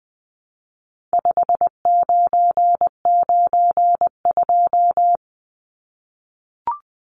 Morse code, heard as a keyed tone, about 20 wpm, 700 Hz. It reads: 5992
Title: Numbers - 4-Digit 20wpm